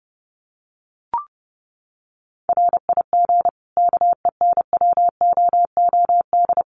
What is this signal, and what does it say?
Morse code, heard as a keyed tone, about 28 wpm, 700 Hz. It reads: RIG KENWOOD
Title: Common QSO Phrases 30wpm